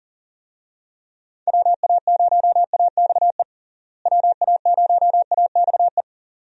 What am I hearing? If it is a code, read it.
Morse code, heard as a keyed tone, about 40 wpm, 700 Hz. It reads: WA0AXE WA0AXE
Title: USA Call Signs - 6 Characters - Repeated 2x - 40wpm